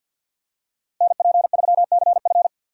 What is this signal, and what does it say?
Morse code, heard as a keyed tone, about 52 wpm, 700 Hz. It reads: NP3CF
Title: USA Call Signs - 5 Characters - Repeated 2x - 50wpm